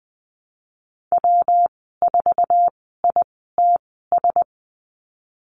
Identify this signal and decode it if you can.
Morse code, heard as a keyed tone, about 20 wpm, spaced as for 15 wpm, 700 Hz. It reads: W4ITS
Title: 122 Lesson 20g - Number 3 - Call Signs 20wpm